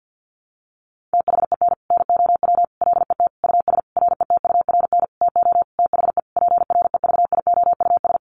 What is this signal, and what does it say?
Morse code, heard as a keyed tone, about 50 wpm, 700 Hz. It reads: THEN NOW LET US RETURN TO THE PREVIOUS
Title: Sentences from Top 1500 Words 50wpm